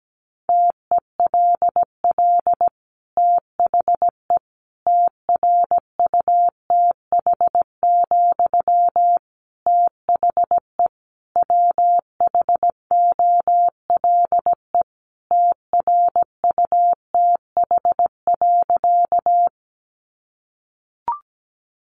Morse code, heard as a keyed tone, about 17 wpm, 700 Hz. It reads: TELL THE TRUTH, THE WHOLE TRUTH.
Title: Sentences from Top 800 Words 17wpm